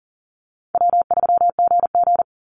Morse code, heard as a keyed tone, about 39 wpm, 700 Hz. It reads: W3GG
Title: Call Signs Repeated 3x - 40wpm